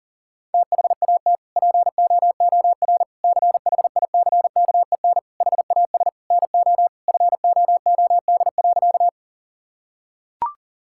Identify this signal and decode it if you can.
Morse code, heard as a keyed tone, about 40 wpm, 700 Hz. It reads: THAT POOR CHICKEN HAS NO FOOD.
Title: Sentences from Top 1000 Words - Review Part 5 - 40wpm